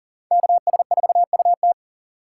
Morse code, heard as a keyed tone, about 40 wpm, 700 Hz. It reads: KS4UT
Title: USA Call Signs - 5 Characters - 3x Character Spacing - 40wpm